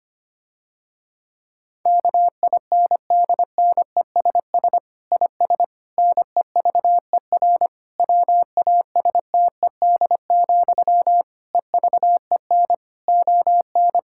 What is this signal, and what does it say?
Morse code, heard as a keyed tone, about 25 wpm, 700 Hz. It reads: KINDNESS IS NEVER WASTED, EVEN ON
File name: practice / 25wpm / Difficult Sentences with Hints - 25wpm